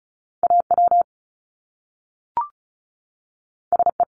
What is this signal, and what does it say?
Morse code, heard as a keyed tone, about 36 wpm, 700 Hz. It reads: AW SE
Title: Top Two Letter Combinations in Words 35wpm